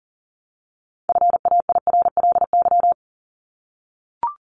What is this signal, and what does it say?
Morse code, heard as a keyed tone, about 40 wpm, 700 Hz. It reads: FAIRLY
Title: Top 1001-1250 Words 40wpm